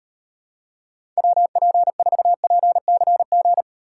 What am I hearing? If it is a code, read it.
Morse code, heard as a keyed tone, about 31 wpm, 700 Hz. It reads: WP4PCG
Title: Speed Racing - USA 2x3 Call Signs 25wpm